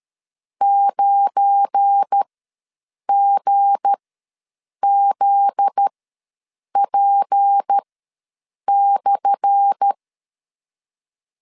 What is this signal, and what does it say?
Morse code, heard as a keyed tone, about 13 wpm, spaced as for 7 wpm, 800 Hz. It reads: 9GZP/
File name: LEZIONI / 10 SIGNS / 10A SLASH ws 12wpm 3443